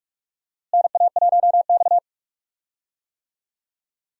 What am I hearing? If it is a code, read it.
Morse code, heard as a keyed tone, about 46 wpm, 700 Hz. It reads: NA1X
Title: USA Call Signs - 4 Characters - Repeated 2x - 45wpm